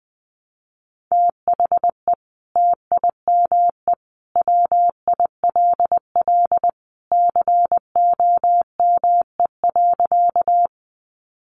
Morse code, heard as a keyed tone, about 20 wpm, 700 Hz. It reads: THE TIME WILL COME.
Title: Sentences from Top 100 Words 20wpm